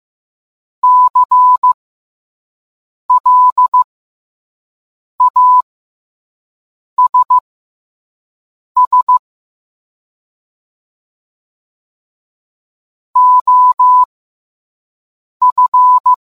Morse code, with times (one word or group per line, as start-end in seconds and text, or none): 0.8-9.2 CLASS
13.2-16.1 OF